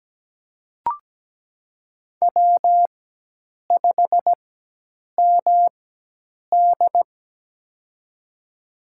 Morse code, heard as a keyed tone, about 17 wpm, spaced as for 8 wpm, 700 Hz. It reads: W5MD
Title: USA Call Signs - 4x Character Spacing - 17wpm